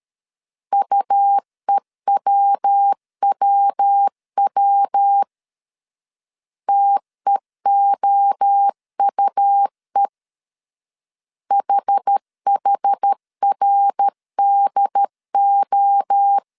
Morse code, heard as a keyed tone, about 13 wpm, 800 Hz. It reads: UEWWW TEOUE HHRDO
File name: CW / DET / 1 3 1500C S C 12wpm 2750